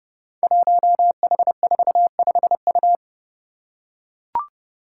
Morse code, heard as a keyed tone, about 30 wpm, 700 Hz. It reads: 1H45U
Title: Five Challenging Letter-Number Combinations 30wpm